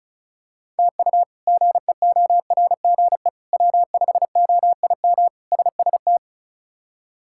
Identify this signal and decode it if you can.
Morse code, heard as a keyed tone, about 34 wpm, 700 Hz. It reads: TU GEORGE W5OIM SST
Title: K1USN SST QSO Practice - 35wpm